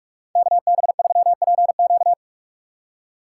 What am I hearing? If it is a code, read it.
Morse code, heard as a keyed tone, about 46 wpm, 700 Hz. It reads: KB3PQ
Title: USA Call Signs - 6x Character Spacing - 45wpm